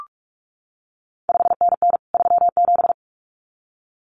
Morse code, heard as a keyed tone, about 46 wpm, 700 Hz. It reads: 5NN 37
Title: CQ WW DX Contest Exchange Encore - 45wpm